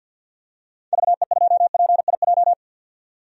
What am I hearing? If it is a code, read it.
Morse code, heard as a keyed tone, about 51 wpm, 700 Hz. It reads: VE2PIJ
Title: DXCC-100 Call Signs - 6 Characters - 50wpm